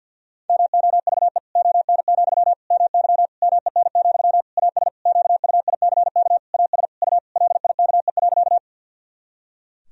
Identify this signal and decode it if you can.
Morse code, heard as a keyed tone, about 48 wpm, 700 Hz. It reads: MOVE ON, MY MEN, AS QUICK AS U LIKE.
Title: Sentences from Top 500 Words and Abbreviations 50wpm